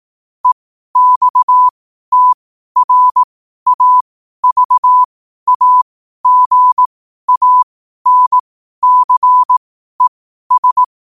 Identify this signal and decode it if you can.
Morse code, heard as a keyed tone, about 18 wpm, spaced as for 12 wpm, 1000 Hz. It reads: EXTRAVAGANCES
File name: qotdfiles / 12wpm